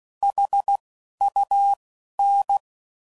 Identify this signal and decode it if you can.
Morse code, heard as a keyed tone, about 16 wpm, 800 Hz. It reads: H U N
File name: CW Code Groups @15WPM / QSOs / WD0BGP Spaced